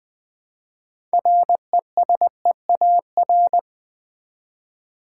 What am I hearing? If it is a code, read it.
Morse code, heard as a keyed tone, about 20 wpm, 700 Hz. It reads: RESEAR
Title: Build A Word - 3 Letters At A Time - 8-Letter Words - 20wpm